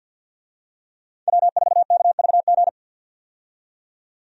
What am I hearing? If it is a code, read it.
Morse code, heard as a keyed tone, about 47 wpm, 700 Hz. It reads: W4KVG
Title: USA Call Signs - 5 Characters - 50wpm